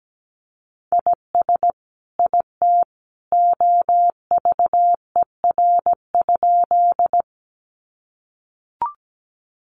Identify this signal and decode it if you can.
Morse code, heard as a keyed tone, about 17 wpm, 700 Hz. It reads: IS IT OVER?
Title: Sentences from Top 100 Words 17wpm